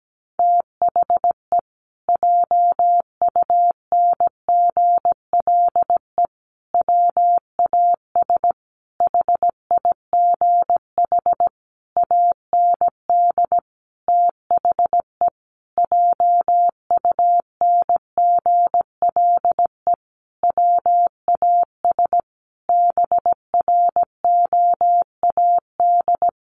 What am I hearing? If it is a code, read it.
Morse code, heard as a keyed tone, about 17 wpm, 700 Hz. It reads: THE JUNGLE WAS HIGH AND THE JUNGLE WAS BROAD
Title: A Sound Of Thunder 17wpm